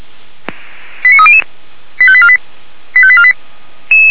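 Five-tone selective call (ZVEI), heard as quick runs of five tones, tones of about 70 ms. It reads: 89300 85848 85848
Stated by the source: ZVEI five-tone call received by SDR.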